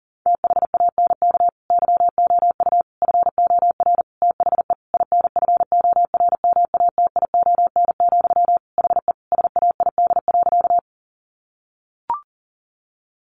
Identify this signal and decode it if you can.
Morse code, heard as a keyed tone, about 40 wpm, 700 Hz. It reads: THANK YOU FOR THE INFORMATION, HE SAID.
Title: Sentences from Top 600 Words 40wpm